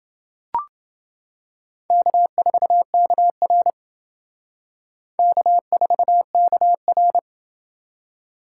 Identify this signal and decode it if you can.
Morse code, heard as a keyed tone, about 29 wpm, 700 Hz. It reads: K4KR K4KR